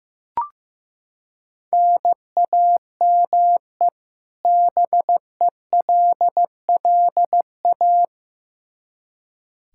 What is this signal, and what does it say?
Morse code, heard as a keyed tone, about 15 wpm, 700 Hz. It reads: NAME BELLA
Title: Practice QSO - Exchange 1 - Beginning - 15wpm